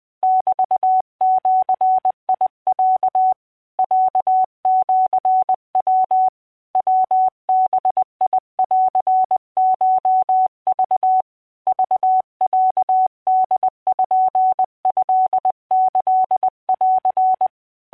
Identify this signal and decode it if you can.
Morse code (Japanese Wabun code), heard as a keyed tone, about 20 wpm, 750 Hz. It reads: メジロ ロシヤ ヤバンコク クロポトキン